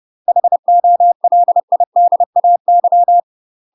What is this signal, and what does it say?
Morse code, heard as a keyed tone, about 30 wpm, 700 Hz. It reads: HOLIDAY
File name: UK MASTER QSOs TRAINING / 30WPM / UK MASTER QSOs 0029